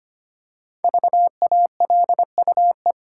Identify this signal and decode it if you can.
Morse code, heard as a keyed tone, about 25 wpm, 700 Hz. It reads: VALUE
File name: 5 Letter Words - First 3 Letters Given - 25wpm